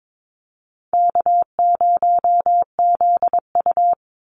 Morse code, heard as a keyed tone, about 22 wpm, 700 Hz. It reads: K0ZU
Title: Easy Call Signs - 22wpm